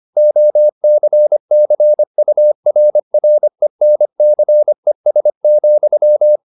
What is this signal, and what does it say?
Morse code, heard as a keyed tone, about 25 wpm, 600 Hz. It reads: OCCURRENCES,
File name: the old man and the sea part 1 25wpm